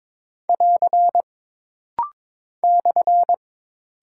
Morse code, heard as a keyed tone, about 22 wpm, 700 Hz. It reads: <AR> /
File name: practice / 22wpm / Single Punctuation-Prosign - Rapid-Fire - 22wpm